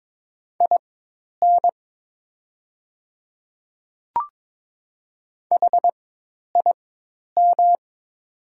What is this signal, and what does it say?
Morse code, heard as a keyed tone, about 22 wpm, spaced as for 10 wpm, 700 Hz. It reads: IN HIM
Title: 4x Character Spacing - Top 100 Words - 22wpm